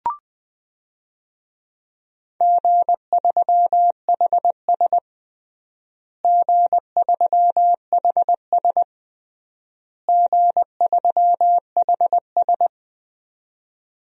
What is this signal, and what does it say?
Morse code, heard as a keyed tone, about 20 wpm, 700 Hz. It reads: G3HS G3HS G3HS
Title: DXCC-100 Call Signs - 4 Characters - Repeated 3x - 20wpm